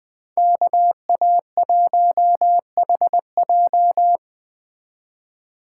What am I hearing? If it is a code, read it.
Morse code, heard as a keyed tone, about 21 wpm, 700 Hz. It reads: KA1HJ